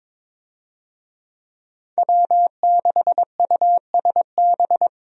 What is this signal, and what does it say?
Morse code, heard as a keyed tone, about 22 wpm, 700 Hz. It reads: W6USB
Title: USA Call Signs - 5 Characters - Repeated 2x - 22wpm